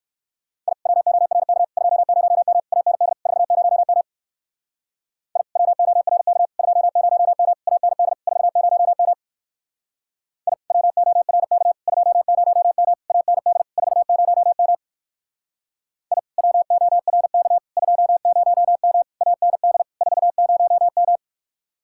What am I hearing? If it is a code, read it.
Morse code, heard as a keyed tone, about 56 wpm, 700 Hz. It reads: I WORK 20M AND 40M I WORK 20M AND 40M I WORK 20M AND 40M I WORK 20M AND 40M
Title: Speed-Racing Common QSO Phrases Encore - 45wpm